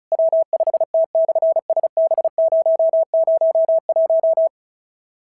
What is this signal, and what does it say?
Morse code, heard as a keyed tone, about 34 wpm, 650 Hz. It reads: W5T/SB001
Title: SOTA QSO Practice 35wpm